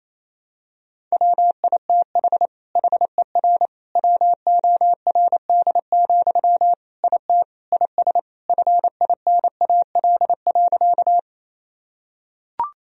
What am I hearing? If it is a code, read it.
Morse code, heard as a keyed tone, about 28 wpm, 700 Hz. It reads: WITH HER WORD, IT IS FINAL.
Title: Sentences from Top 1000 Words - Review Part 2 - 28wpm